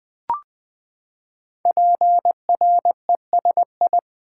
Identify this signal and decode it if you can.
Morse code, heard as a keyed tone, about 20 wpm, 700 Hz. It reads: PRESI
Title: Build A Word - 5 Letters At A Time - 12-Letter Words - 20wpm